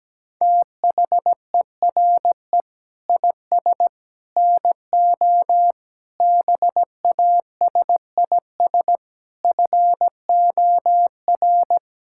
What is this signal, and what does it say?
Morse code, heard as a keyed tone, about 17 wpm, 700 Hz. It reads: THERE IS NO BASIS FOR
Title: Sentences from Top 1600 Words 17wpm